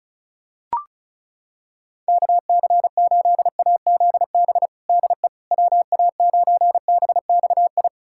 Morse code, heard as a keyed tone, about 36 wpm, 700 Hz. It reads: KC8AZB DE WA9BXI
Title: K1USN SST QSO Practice 35wpm